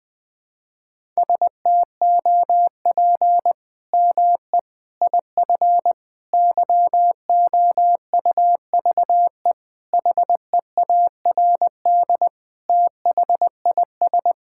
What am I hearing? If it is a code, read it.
Morse code, heard as a keyed tone, about 20 wpm, 700 Hz. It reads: STOP ME IF YOUVE HEARD THIS
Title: Sets of 8 Words - 20wpm